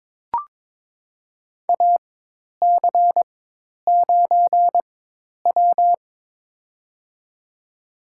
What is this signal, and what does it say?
Morse code, heard as a keyed tone, about 22 wpm, spaced as for 10 wpm, 700 Hz. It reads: AC9W
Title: USA Call Signs - 4 Characters - 4x Character Spacing - 22wpm